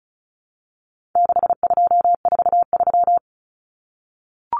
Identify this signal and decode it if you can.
Morse code, heard as a keyed tone, about 34 wpm, 700 Hz. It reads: 6243
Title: Numbers - 4-Digit 35wpm